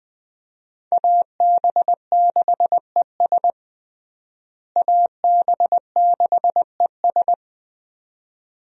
Morse code, heard as a keyed tone, about 21 wpm, 700 Hz. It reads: AB6ES AB6ES
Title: USA Call Signs - 5 Characters - Repeated 2x - 20wpm